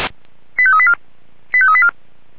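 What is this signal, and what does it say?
Five-tone selective call (ZVEI), heard as quick runs of five tones, tones of about 70 ms. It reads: 85384 85384
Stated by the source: ZVEI five-tone selective call received on an SDR.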